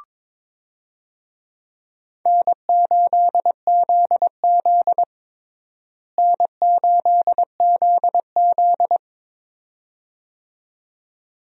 Morse code, heard as a keyed tone, about 22 wpm, 700 Hz. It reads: N8ZZ N8ZZ
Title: USA Call Signs - 4 Characters - Repeated 2x - 22wpm